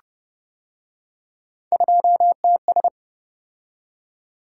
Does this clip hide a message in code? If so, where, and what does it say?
Yes: Morse code from 1.7 to 2.9 s, heard as a keyed tone, about 29 wpm, 700 Hz. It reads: 2TS